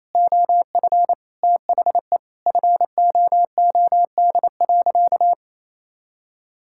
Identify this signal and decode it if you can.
Morse code, heard as a keyed tone, about 28 wpm, 700 Hz. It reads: OF THE FOOD.
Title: Sentences from Top 1000 Words - Review Part 6 - 28wpm